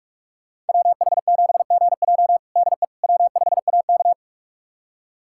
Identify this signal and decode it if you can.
Morse code, heard as a keyed tone, about 43 wpm, 700 Hz. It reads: WH7GJ DE W5AK
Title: USA Callsign DE Callsign 45wpm